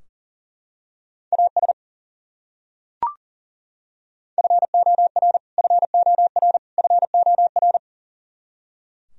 Morse code, heard as a keyed tone, about 40 wpm, 700 Hz. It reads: AS FOR FOR FOR
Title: Top 100 Words Repeated 40wpm